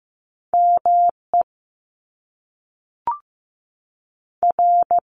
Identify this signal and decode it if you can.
Morse code, heard as a keyed tone, about 15 wpm, 700 Hz. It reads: M E R
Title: Top Two Letter Combinations in Words 15wpm